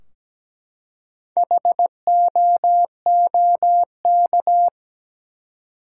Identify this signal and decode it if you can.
Morse code, heard as a keyed tone, about 17 wpm, 700 Hz. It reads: HOOK
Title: Four Letter Words - 17wpm